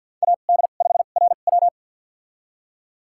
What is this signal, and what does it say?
Morse code, heard as a keyed tone, about 51 wpm, spaced as for 36 wpm, 700 Hz. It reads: AD5RW
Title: USA Call Signs - 5 Characters - 2x Character Spacing - 50wpm